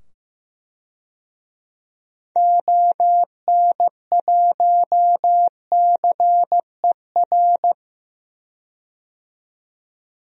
Morse code, heard as a keyed tone, about 15 wpm, 700 Hz. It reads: ON1CER
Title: DXCC-100 Call Signs - 6 Characters - 15wpm